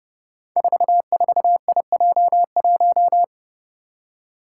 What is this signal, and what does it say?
Morse code, heard as a keyed tone, about 28 wpm, 700 Hz. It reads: 44IJ1
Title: Five Challenging Letter-Number Combinations 30wpm